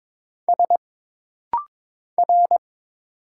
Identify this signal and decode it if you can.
Morse code, heard as a keyed tone, about 22 wpm, 700 Hz. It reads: S R
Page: This rapid-fire timing is geared to push your ICR (Instant Character Recognition) skills.